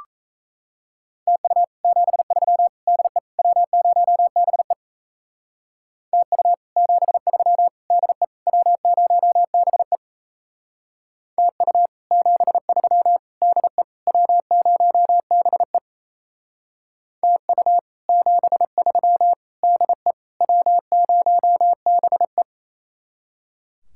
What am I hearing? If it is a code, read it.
Morse code, heard as a keyed tone, about 36 wpm, 700 Hz. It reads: TU 73 DE W0BE TU 73 DE W0BE TU 73 DE W0BE TU 73 DE W0BE